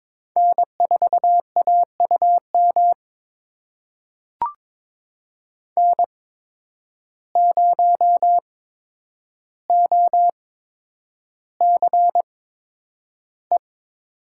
Morse code, heard as a keyed tone, about 22 wpm, spaced as for 9 wpm, 700 Hz. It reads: N4AUM N0OCE
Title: USA Call Signs - 5 Characters - 8x Character Spacing - 22wpm